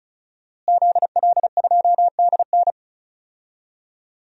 Morse code, heard as a keyed tone, about 34 wpm, 700 Hz. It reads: ZL2DN
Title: DXCC-100 Call Signs - 5 Characters - Repeated 2x - 35wpm